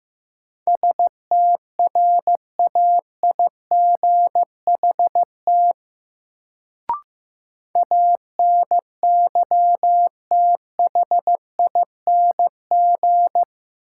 Morse code, heard as a keyed tone, about 15 wpm, 700 Hz. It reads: STRAIGHT ANYTHING
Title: Sentences from Top 1000 Words - New Word Drill - 15wpm